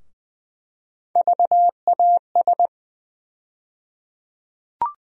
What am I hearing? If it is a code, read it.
Morse code, heard as a keyed tone, about 20 wpm, 700 Hz. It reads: VAS
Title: USA Call Sign Postfixes - 3 Letters 20wpm